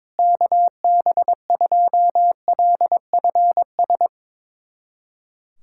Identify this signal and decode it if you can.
Morse code, heard as a keyed tone, about 22 wpm, 700 Hz. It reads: KB2LFS
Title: USA Callsign DE Callsign - 6 Characters - 22wpm